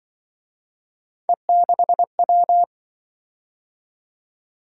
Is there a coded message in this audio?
Yes: Morse code from 1.3 to 2.6 s, heard as a keyed tone, about 26 wpm, 700 Hz. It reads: E6W